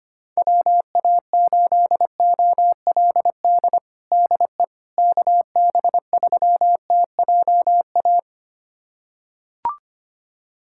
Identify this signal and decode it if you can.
Morse code, heard as a keyed tone, about 25 wpm, 700 Hz. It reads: WA8OLD DE KB3TJA
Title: USA Callsign DE Callsign - 6 Characters - 25wpm